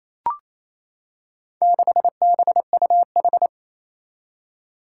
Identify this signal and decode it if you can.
Morse code, heard as a keyed tone, about 28 wpm, 700 Hz. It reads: 6BUH
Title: Four Challenging Letter-Number Combinations 28wpm